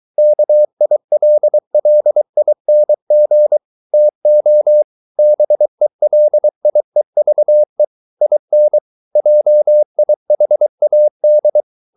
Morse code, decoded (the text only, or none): KILLING TO BELIEVE IN JIHAD